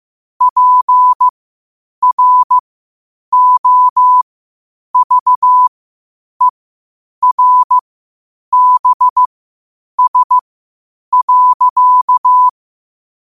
Morse code, heard as a keyed tone, about 15 wpm, spaced as for 8 wpm, 1000 Hz. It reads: PROVERBS.
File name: qotdfiles / 8wpm